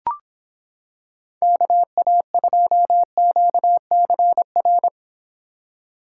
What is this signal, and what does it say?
Morse code, heard as a keyed tone, about 26 wpm, 700 Hz. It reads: KA2QCR